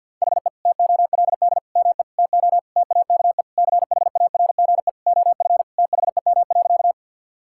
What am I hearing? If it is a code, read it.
Morse code, heard as a keyed tone, about 50 wpm, 700 Hz. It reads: HE TOLD ME TO TAKE CHARGE OF THEM.
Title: Sentences from Top 1000 Words - Review Part 6 - 50wpm